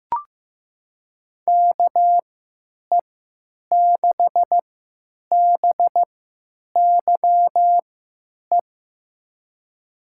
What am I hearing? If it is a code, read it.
Morse code, heard as a keyed tone, about 15 wpm, spaced as for 9 wpm, 700 Hz. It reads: KE6BYE
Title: USA Call Signs - 6 Characters - 3x Character Spacing - 15wpm